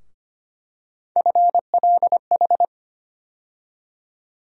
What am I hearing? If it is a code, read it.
Morse code, heard as a keyed tone, about 25 wpm, 700 Hz. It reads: FLH
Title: USA Call Sign Postfixes - 3 Letters 25wpm